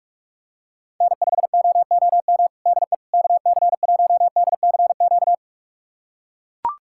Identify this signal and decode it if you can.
Morse code, heard as a keyed tone, about 41 wpm, 700 Hz. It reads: N5OOM DE KC1DCQ
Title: USA Callsign DE Callsign 45wpm